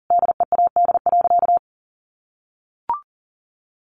Morse code, heard as a keyed tone, about 40 wpm, 700 Hz. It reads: DEAD.
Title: Sentences from Top 1500 Words 40wpm